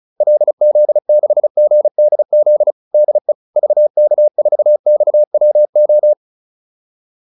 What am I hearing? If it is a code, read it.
Morse code, heard as a keyed tone, about 36 wpm, 600 Hz. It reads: LZ6GDZ DE VK4XWO
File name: QSO407 35